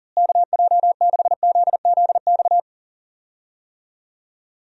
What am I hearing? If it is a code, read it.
Morse code, heard as a keyed tone, about 40 wpm, 700 Hz. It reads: KJ6ZZX
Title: USA Call Signs - 6 Characters - Repeated 2x - 40wpm